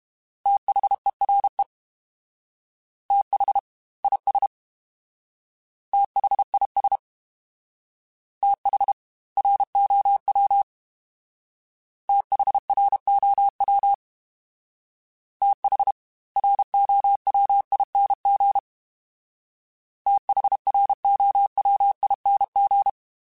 Morse code, heard as a keyed tone, about 32 wpm, 800 Hz. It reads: THERE TH IS THIS TH ROW THROW TH ROWING THROWING
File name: TH / TH mp3s / TH Prefixes spt @30